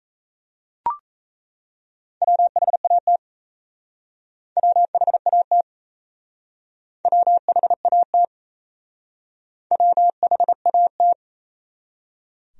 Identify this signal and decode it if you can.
Morse code, heard as a keyed tone, about 37 wpm, 700 Hz. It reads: WHAT WHAT WHAT WHAT